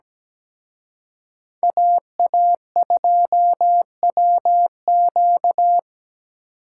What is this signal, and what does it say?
Morse code, heard as a keyed tone, about 20 wpm, 700 Hz. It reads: AA2WQ
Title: Speed Racing - USA 2x2 Call Signs 17wpm